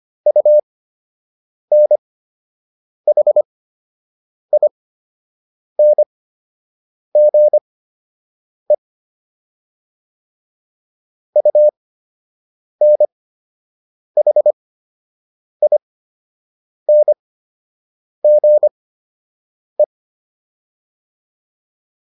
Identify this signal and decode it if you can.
Morse code, heard as a keyed tone, about 25 wpm, spaced as for 7 wpm, 600 Hz. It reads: UNHINGE UNHINGE